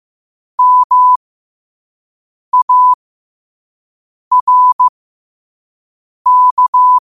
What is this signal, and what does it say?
Morse code, heard as a keyed tone, about 15 wpm, spaced as for 5 wpm, 1000 Hz. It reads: MARK
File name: qotdfiles / 5wpm